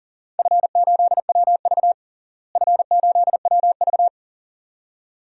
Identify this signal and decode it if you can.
Morse code, heard as a keyed tone, about 40 wpm, 700 Hz. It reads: F8WV F8WV
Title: DXCC-100 Call Signs - 4 Characters - Repeated 2x - 40wpm